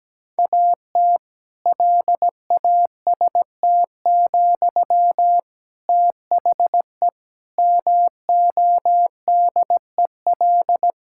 Morse code, heard as a keyed tone, about 17 wpm, 700 Hz. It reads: AT LAST, THE MODEL
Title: Sentences from Top 1000 Words - Review Part 5 - 17wpm